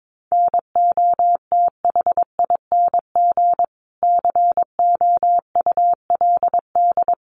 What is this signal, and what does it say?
Morse code, heard as a keyed tone, about 22 wpm, 700 Hz. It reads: NOTHING COULD
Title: Sentences from Top 500 Words 22wpm